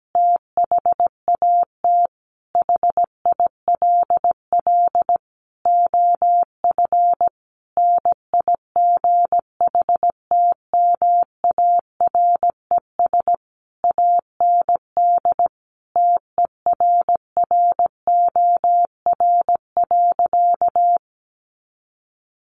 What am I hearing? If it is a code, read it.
Morse code, heard as a keyed tone, about 17 wpm, 700 Hz. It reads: THAT HILL OF NIGHTMARES AND TERROR.
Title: A Sound Of Thunder 17wpm